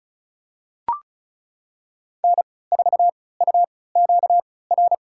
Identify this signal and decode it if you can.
Morse code, heard as a keyed tone, about 36 wpm, spaced as for 20 wpm, 700 Hz. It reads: N4UQR
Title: USA Call Signs - 5 Characters - 3x Character Spacing - 35wpm